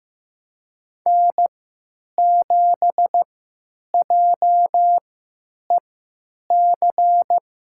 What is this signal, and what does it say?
Morse code, heard as a keyed tone, about 15 wpm, spaced as for 9 wpm, 700 Hz. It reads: N7JEC
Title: USA Call Signs - 5 Characters - 3x Character Spacing - 15wpm